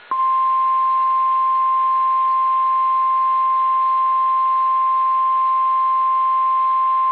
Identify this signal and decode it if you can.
Emergency Alert System (SAME) tones (the 1050 Hz attention tone). It is heard from 1.0 to 6.0 s.
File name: eas / KSDZKDJL / KSDZKDJL